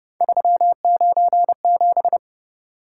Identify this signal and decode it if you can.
Morse code, heard as a keyed tone, about 29 wpm, 700 Hz. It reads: 397